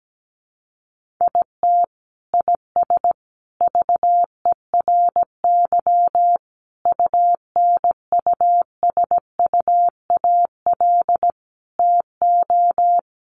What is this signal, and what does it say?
Morse code, heard as a keyed tone, about 17 wpm, 700 Hz. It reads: IT IS VERY UNUSUAL TO
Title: Sentences from Top 1400 Words 17wpm